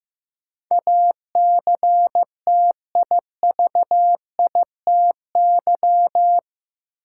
Morse code, heard as a keyed tone, about 15 wpm, 700 Hz. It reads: ACTIVITY
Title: Sentences from Top 1000 Words - Review Part 4 - New Word Drill - 15wpm